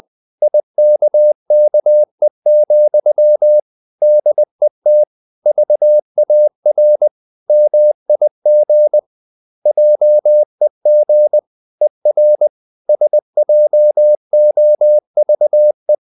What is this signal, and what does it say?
Morse code, heard as a keyed tone, about 20 wpm, 600 Hz. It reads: IKKE, DET VAR MIG JEG ER SJOVE
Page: Ferie 20 wpm